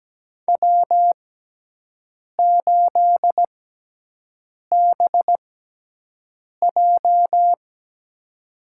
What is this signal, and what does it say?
Morse code, heard as a keyed tone, about 17 wpm, spaced as for 6 wpm, 700 Hz. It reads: W8BJ
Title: USA Call Signs - 4 Characters - 6x Character Spacing - 17wpm